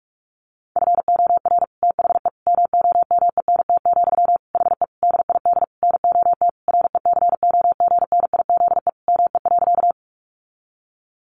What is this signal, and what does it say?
Morse code, heard as a keyed tone, about 44 wpm, 700 Hz. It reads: FOR THE MOMENT, HE DID NOT RECOGNIZE ME.
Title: Sentences from Top 1400 Words 45wpm